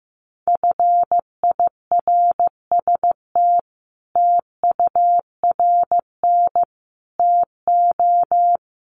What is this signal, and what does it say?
Morse code, heard as a keyed tone, about 15 wpm, 700 Hz. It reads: FIRST TURN TO
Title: Sentences from Top 300 Words 15wpm